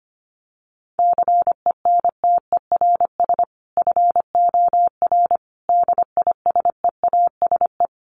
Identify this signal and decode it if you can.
Morse code, heard as a keyed tone, about 25 wpm, 700 Hz. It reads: CENTERS FOR DISEASE
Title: Sets of 5 Words Encore2 25wpm